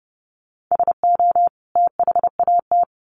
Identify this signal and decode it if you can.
Morse code, heard as a keyed tone, about 30 wpm, 700 Hz. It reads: SO THAT
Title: Sets of 2 Words 30wpm